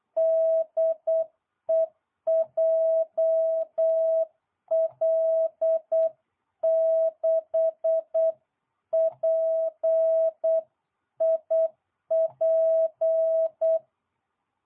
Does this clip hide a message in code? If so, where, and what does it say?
Yes: Morse code from 0.2 to 13.8 s, heard as a keyed tone, about 8 wpm, 660 Hz. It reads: DE JL6PIP